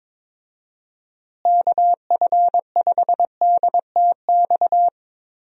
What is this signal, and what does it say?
Morse code, heard as a keyed tone, about 21 wpm, 700 Hz. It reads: KF5DTX
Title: USA Call Signs - 6 Characters - Repeated 3x - 22wpm